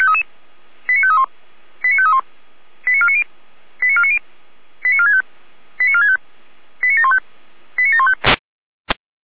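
Five-tone selective call (ZVEI), heard as quick runs of five tones, tones of about 70 ms. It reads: 89531 89531 89490 89490 89465 89465 89715 89715
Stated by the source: ZVEI five-tone selective call received by SDR.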